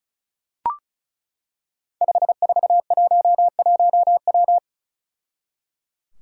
Morse code, heard as a keyed tone, about 36 wpm, 700 Hz. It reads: 5411W
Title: Five Challenging Letter-Number Combinations 35wpm